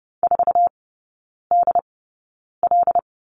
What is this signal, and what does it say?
Morse code, heard as a keyed tone, about 30 wpm, spaced as for 9 wpm, 700 Hz. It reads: 4DL
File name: course / 30:10wpm / 087 Lesson 15e - Number 5 - Three Characters 30wpm